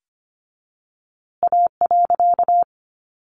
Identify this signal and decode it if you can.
Morse code, heard as a keyed tone, about 24 wpm, 700 Hz. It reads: A.
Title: A Sound Of Thunder 25wpm